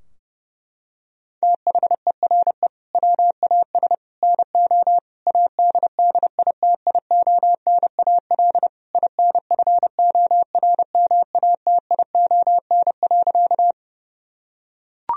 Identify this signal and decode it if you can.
Morse code, heard as a keyed tone, about 32 wpm, 700 Hz. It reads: THERE WAS NO ADDITIONAL INFORMATION.
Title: Sentences from Top 1000 Words - Review Part 4 - 30wpm